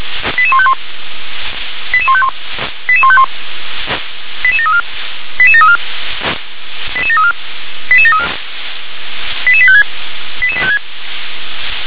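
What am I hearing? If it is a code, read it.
Five-tone selective call (ZVEI), heard as quick runs of five tones, tones of about 70 ms. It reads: 88151 88151 88151 88734 88734 88734 88734 88856 88856
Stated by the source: ZVEI five-tone selective call received by SDR.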